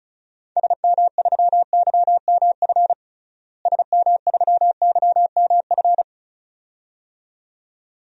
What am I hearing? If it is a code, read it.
Morse code, heard as a keyed tone, about 36 wpm, 700 Hz. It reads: SM3YMF SM3YMF
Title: DXCC-100 Call Signs - 6 Characters - Repeated 2x - 35wpm